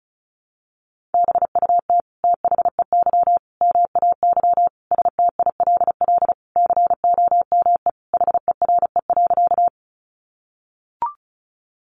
Morse code, heard as a keyed tone, about 34 wpm, 700 Hz. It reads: BUT THEY MAY STILL COME HERE.
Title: Sentences from Top 300 Words 35wpm